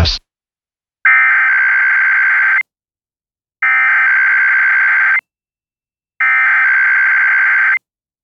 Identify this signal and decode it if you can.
Emergency Alert System (SAME) tones (header bursts), heard as harsh buzzing bursts. It is heard from 1.1 to 7.2 s.